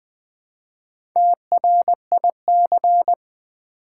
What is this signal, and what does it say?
Morse code, heard as a keyed tone, about 20 wpm, 700 Hz. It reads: TRIC
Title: Build A Word - 1 Letter At A Time - 5-Letter Words - 20wpm